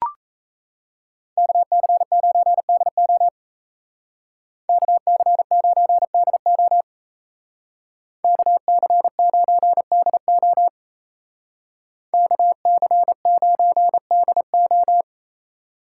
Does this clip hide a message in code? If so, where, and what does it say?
Yes: Morse code from 1.4 to 15.0 s, heard as a keyed tone, about 36 wpm, 700 Hz. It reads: KC9DO KC9DO KC9DO KC9DO